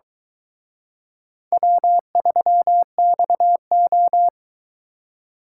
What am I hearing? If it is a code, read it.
Morse code, heard as a keyed tone, about 23 wpm, 700 Hz. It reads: W3XO